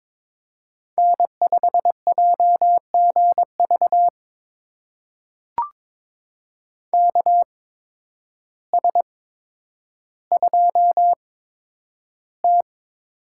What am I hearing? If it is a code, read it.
Morse code, heard as a keyed tone, about 22 wpm, spaced as for 10 wpm, 700 Hz. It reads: N5JGV KS2T
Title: USA Call Signs - 8x Character Spacing - 22wpm